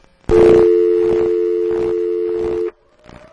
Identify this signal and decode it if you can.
Dial tone (North America), 350 + 440 Hz.